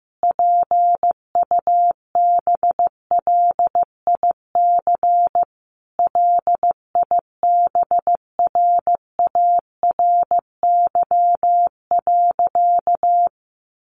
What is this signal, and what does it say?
Morse code, heard as a keyed tone, about 15 wpm, 700 Hz. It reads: PUBLIC LIBRARY.
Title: Sentences from Top 1300 Words 15wpm